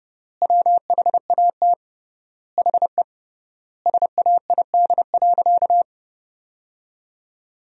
Morse code, heard as a keyed tone, about 31 wpm, 700 Hz. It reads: WHAT HE SAID.
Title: 3x Word-Spacing - Sentences from Top 400 Words - 30wpm